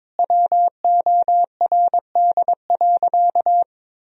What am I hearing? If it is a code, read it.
Morse code, heard as a keyed tone, about 22 wpm, 700 Hz. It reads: WORD.
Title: Sentences from Top 1000 Words - Review Part 4 - 22wpm